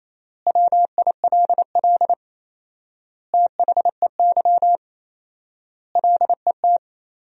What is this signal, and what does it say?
Morse code, heard as a keyed tone, about 28 wpm, 700 Hz. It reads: WILL THEY LET
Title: 4x Word-Spacing - Sentences from Top 400 Words - 28wpm